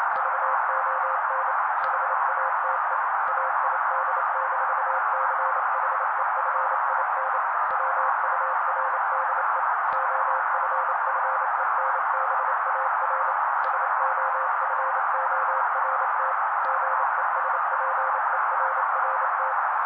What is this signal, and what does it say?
Morse code, heard as a keyed tone, about 28 wpm, 530 Hz. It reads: VON HATE AID-CHEFIN WURDE OFFENBAR SOFORT GESPERRT
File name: daily / 27wpm noise